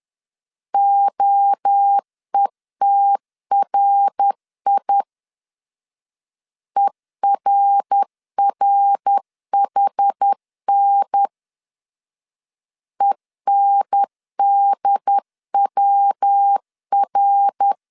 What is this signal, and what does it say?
Morse code, heard as a keyed tone, about 10 wpm, 800 Hz. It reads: OETRI ERRHN ENDWR
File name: CW / DET / 1 3 1500C S C 10wpm 3335